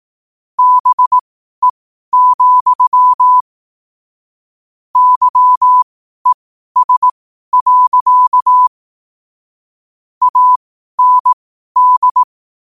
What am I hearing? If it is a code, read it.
Morse code, heard as a keyed tone, about 18 wpm, spaced as for 12 wpm, 1000 Hz. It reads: BE, YES. AND